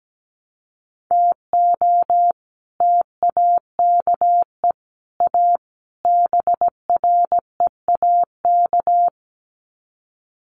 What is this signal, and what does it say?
Morse code, heard as a keyed tone, about 17 wpm, 700 Hz. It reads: TO TAKE A BREAK
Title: Sets of 4 Words 17wpm